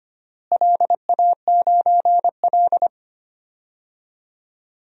Morse code, heard as a keyed tone, about 25 wpm, 700 Hz. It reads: LA9L
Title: DXCC-100 Call Signs - 4 Characters - Repeated 2x - 25wpm